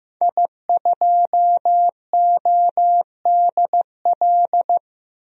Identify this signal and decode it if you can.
Morse code, heard as a keyed tone, about 15 wpm, 700 Hz. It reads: I2ODL
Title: DXCC-100 Call Signs - 5 Characters - Repeated 3x - 15wpm